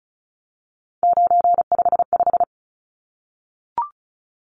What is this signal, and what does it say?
Morse code, heard as a keyed tone, about 34 wpm, 700 Hz. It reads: 955
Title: Numbers - 3-Digit 35wpm